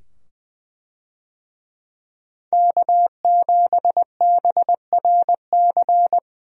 Morse code, heard as a keyed tone, about 20 wpm, 700 Hz. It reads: K7BRC
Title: USA Call Signs - 5 Characters - 20wpm